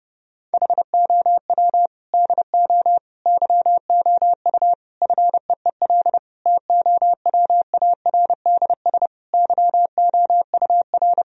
Morse code, heard as a keyed tone, about 30 wpm, 700 Hz. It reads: HOW DO YOU FEEL TOWARDS YOUR
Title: Sentences from Top 1000 Words - Review Part 2 - 30wpm